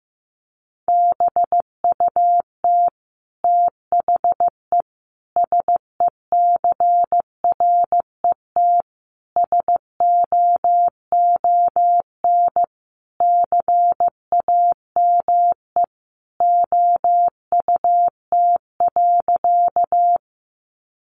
Morse code, heard as a keyed tone, about 15 wpm, 700 Hz. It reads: BUT THE SECRET SOON CAME OUT.
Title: Sentences from Top 1200 Words 15wpm